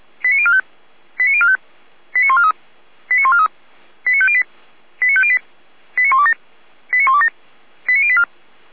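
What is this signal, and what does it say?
Five-tone selective call (ZVEI), heard as quick runs of five tones, tones of about 70 ms. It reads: 89045 89045 89143 89143 89508 89508 89127 89127 89084
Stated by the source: ZVEI five-tone selective call received by SDR.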